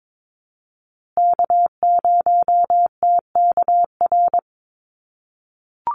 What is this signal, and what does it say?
Morse code, heard as a keyed tone, about 22 wpm, 700 Hz. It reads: K0TKR
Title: USA Call Signs Repeated 2x 22wpm